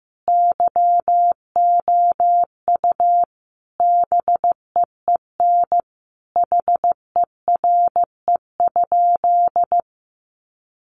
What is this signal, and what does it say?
Morse code, heard as a keyed tone, about 15 wpm, 700 Hz. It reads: YOU BEEN HERE?
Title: Sentences from Top 200 Words 15wpm